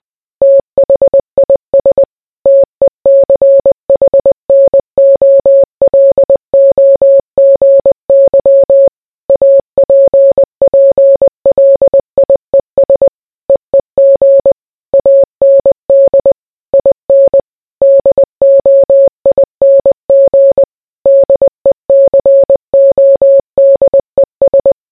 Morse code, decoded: THIS TECHNOLOGY APPLIES EEG AND IN DOING DECODES